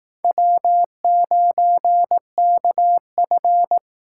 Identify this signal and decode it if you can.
Morse code, heard as a keyed tone, about 19 wpm, 700 Hz. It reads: W9KF